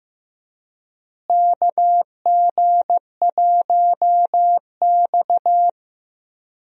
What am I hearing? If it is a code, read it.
Morse code, heard as a keyed tone, about 15 wpm, 700 Hz. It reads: KG1X
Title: USA Call Signs - 4 Characters - Repeated 2x - 15wpm